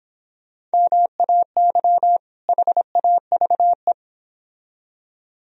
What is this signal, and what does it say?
Morse code, heard as a keyed tone, about 26 wpm, 700 Hz. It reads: MAY HAVE